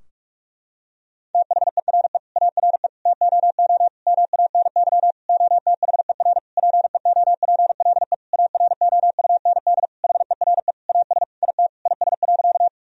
Morse code, heard as a keyed tone, about 42 wpm, 700 Hz. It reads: THERE ARE TOO MANY OTHER PEOPLE AROUND HERE AS IT IS.
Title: Sentences from Top 1000 Words - Review Part 3 - 45wpm